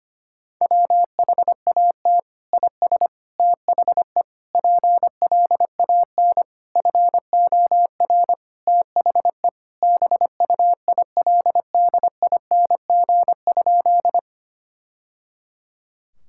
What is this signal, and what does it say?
Morse code, heard as a keyed tone, about 25 wpm, 700 Hz. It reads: WHAT IS THE PLAN FOR THE BUILDING?
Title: Sentences from Top 1000 Words - Review Part 5 - 25wpm